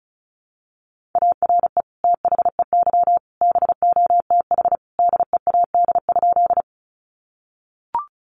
Morse code, heard as a keyed tone, about 34 wpm, 700 Hz. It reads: ARE THEY BOTH DEAD?
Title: Sentences from Top 1500 Words 35wpm